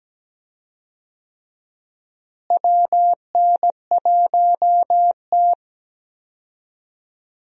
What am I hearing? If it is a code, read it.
Morse code, heard as a keyed tone, about 17 wpm, 700 Hz. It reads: WN1T
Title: USA Call Signs - 4 Characters - Repeated 3x - 17wpm